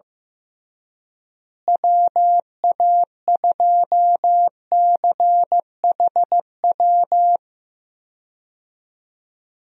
Morse code, heard as a keyed tone, about 15 wpm, 700 Hz. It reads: WA2CHW
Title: USA Call Signs - 6 Characters - Repeated 3x - 15wpm